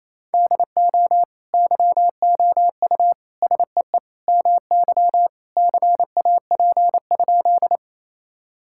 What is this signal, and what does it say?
Morse code, heard as a keyed tone, about 27 wpm, 700 Hz. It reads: DO YOU SEE MY CAP?
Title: Sentences from Top 1600 Words 28wpm